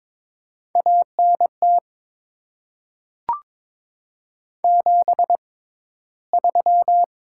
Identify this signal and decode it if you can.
Morse code, heard as a keyed tone, about 22 wpm, spaced as for 13 wpm, 700 Hz. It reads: ANT 73
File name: Top 20 QSO Elements - 6x Character Spacing - 22wpm